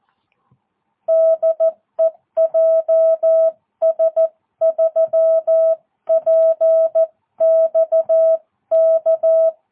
Morse code, heard as a keyed tone, about 14 wpm, 650 Hz. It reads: DE JS3PXK
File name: jh8cbh / 2022cw